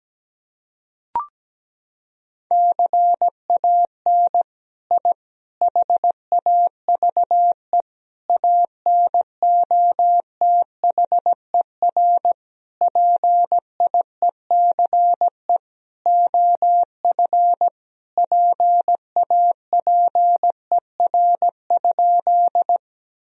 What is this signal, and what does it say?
Morse code, heard as a keyed tone, about 17 wpm, 700 Hz. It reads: CAN I HAVE ANOTHER PIECE OF PAPER?
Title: Sentences from Top 1000 Words - Review Part 2 - 17wpm